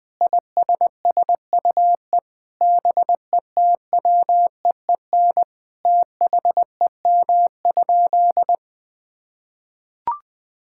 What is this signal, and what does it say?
Morse code, heard as a keyed tone, about 21 wpm, 700 Hz. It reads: ISSUE BETWEEN THEM?
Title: Sentences from Top 1000 Words - Review Part 6 - 20wpm